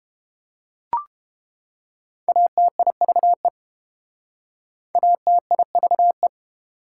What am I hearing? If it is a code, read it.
Morse code, heard as a keyed tone, about 31 wpm, 700 Hz. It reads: ATIVE ATIVE